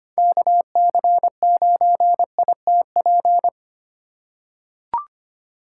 Morse code, heard as a keyed tone, about 25 wpm, 700 Hz. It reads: KC9ITP
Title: USA Call Signs - 6 Characters - 3x Character Spacing - 25wpm